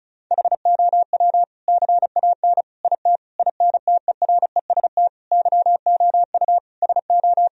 Morse code, heard as a keyed tone, about 34 wpm, 700 Hz. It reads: HOW CAN IT INTEREST YOU SO
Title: Sentences from Top 1000 Words - Review Part 2 - 35wpm